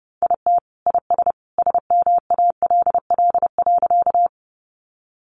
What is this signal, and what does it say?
Morse code, heard as a keyed tone, about 29 wpm, 700 Hz. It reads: IT IS SMALL.
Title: Additional Easy Phrases 30wpm